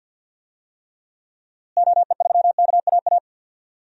Morse code, heard as a keyed tone, about 48 wpm, 700 Hz. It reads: OE3KAA
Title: DXCC-100 Call Signs - 6 Characters - Repeated 2x - 50wpm